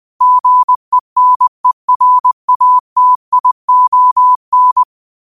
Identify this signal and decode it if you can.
Morse code, heard as a keyed tone, about 20 wpm, 1000 Hz. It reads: GENERATION